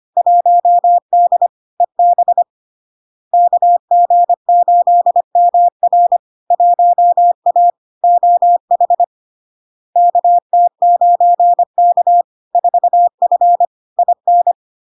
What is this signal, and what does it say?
Morse code, heard as a keyed tone, about 25 wpm, 700 Hz. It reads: 1D EB KG8MR 1A OH KT9K 4F IN
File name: Log-in-CW-25wpm-700-Hz